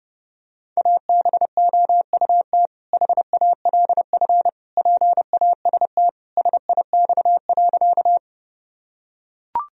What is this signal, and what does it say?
Morse code, heard as a keyed tone, about 30 wpm, 700 Hz. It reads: ABOUT HALF PAST SIX.
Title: Sentences from Top 1000 Words - Review Part 3 - 30wpm